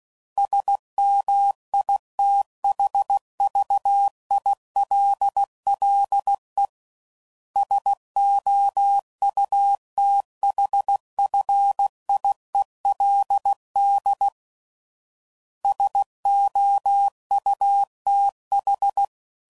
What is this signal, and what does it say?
Morse code, heard as a keyed tone, about 16 wpm, 800 Hz. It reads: SMITHVILLE SOUTHFIELD SOUTH
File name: CW Code Groups @15WPM / fromSM's / Cities